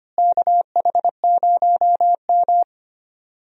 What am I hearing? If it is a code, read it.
Morse code, heard as a keyed tone, about 25 wpm, 700 Hz. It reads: KH0M
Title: USA Call Signs - 4 Characters - Repeated 2x - 25wpm